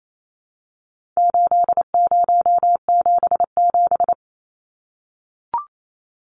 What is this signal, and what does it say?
Morse code, heard as a keyed tone, about 28 wpm, 700 Hz. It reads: 8077
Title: Numbers - 4-Digit 28wpm